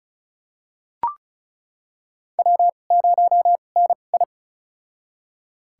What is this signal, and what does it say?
Morse code, heard as a keyed tone, about 34 wpm, spaced as for 25 wpm, 700 Hz. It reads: W0NI
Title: USA Call Signs - 4 Characters - 2x Character Spacing - 35wpm